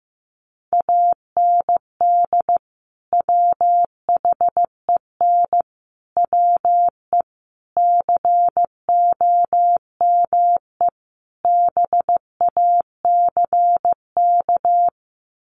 Morse code, heard as a keyed tone, about 15 wpm, 700 Hz. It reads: AND WHEN WE COME BACK
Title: Sets of 5 Words 15wpm